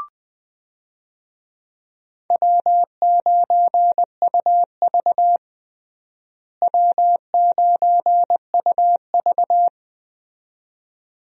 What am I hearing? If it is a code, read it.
Morse code, heard as a keyed tone, about 21 wpm, 700 Hz. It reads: W9UV W9UV
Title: USA Call Signs - 4 Characters - Repeated 2x - 20wpm